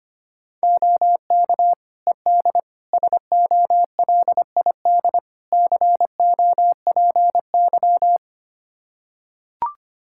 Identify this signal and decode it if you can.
Morse code, heard as a keyed tone, about 25 wpm, 700 Hz. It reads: OK ED SOLID COPY
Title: K1USN SST QSO Practice 25wpm